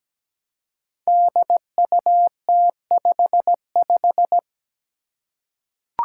Morse code, heard as a keyed tone, about 17 wpm, 700 Hz. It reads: DUT55
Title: Five Challenging Letter-Number Combinations 17wpm